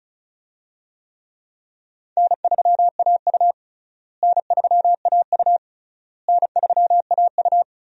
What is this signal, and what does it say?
Morse code, heard as a keyed tone, about 36 wpm, 700 Hz. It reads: N3AU N3AU N3AU
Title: USA Call Signs - 4 Characters - Repeated 3x - 35wpm